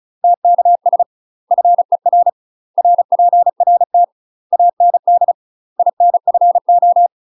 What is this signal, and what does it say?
Morse code, heard as a keyed tone, about 35 wpm, 700 Hz. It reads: TKS FER RPRT AND INFO
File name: UK MASTER QSOs TRAINING / 35WPM / UK MASTER QSOs 0005